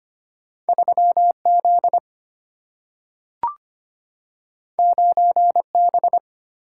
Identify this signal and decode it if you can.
Morse code, heard as a keyed tone, about 25 wpm, 700 Hz. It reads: 3Z 9B
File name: Two Letter-Number Combinations - 25wpm